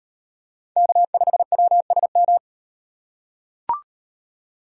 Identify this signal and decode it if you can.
Morse code, heard as a keyed tone, about 38 wpm, 700 Hz. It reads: K5WSM